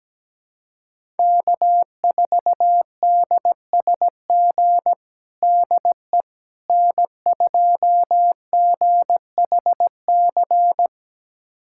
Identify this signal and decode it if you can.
Morse code, heard as a keyed tone, about 17 wpm, 700 Hz. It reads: K4DSG DE N2GHC
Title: USA Callsign DE Callsign - 5 Characters - 17wpm